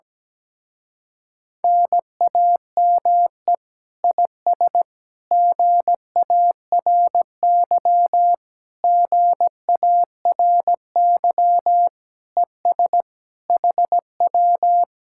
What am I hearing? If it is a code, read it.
Morse code, heard as a keyed tone, about 17 wpm, 700 Hz. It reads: NAME IS GARY GARY ES HW
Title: Speed-Racing Common QSO Phrases Encore - 17wpm